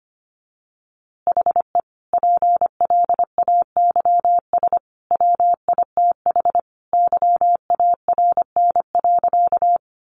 Morse code, heard as a keyed tone, about 25 wpm, 700 Hz. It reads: HE PLAYS WITH YARN.
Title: Easy Phrases 25wpm